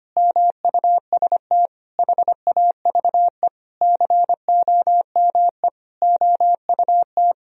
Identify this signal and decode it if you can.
Morse code, heard as a keyed tone, about 25 wpm, 700 Hz. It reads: MUST HAVE COME OUT
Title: Sentences from Top 1000 Words - Review Part 2 - 25wpm